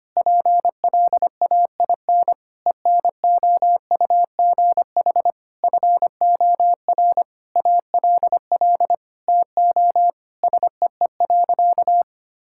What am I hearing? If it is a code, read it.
Morse code, heard as a keyed tone, about 25 wpm, 700 Hz. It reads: PLAIN ENOUGH FOR ALL TO SEE.
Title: Sentences from Top 1000 Words - Review Part 5 - 25wpm